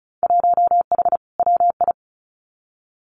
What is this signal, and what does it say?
Morse code, heard as a keyed tone, about 34 wpm, 700 Hz. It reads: 1H WI
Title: Winter Field Day Exchanges 35wpm